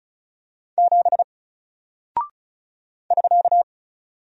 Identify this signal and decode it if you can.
Morse code, heard as a keyed tone, about 36 wpm, 700 Hz. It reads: 7 <SK>
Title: Single Character (Letter:Number:Punctuation:Prosign) Alt 35wpm